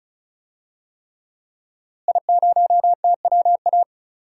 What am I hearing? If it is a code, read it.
Morse code, heard as a keyed tone, about 36 wpm, 700 Hz. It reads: I0TWA
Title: DXCC-100 Call Signs - 5 Characters - Repeated 3x - 35wpm